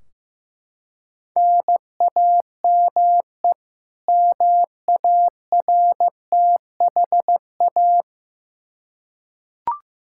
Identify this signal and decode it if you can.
Morse code, heard as a keyed tone, about 15 wpm, 700 Hz. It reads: NAME MARTHA
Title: Practice QSO - Exchange 1 - Beginning - 15wpm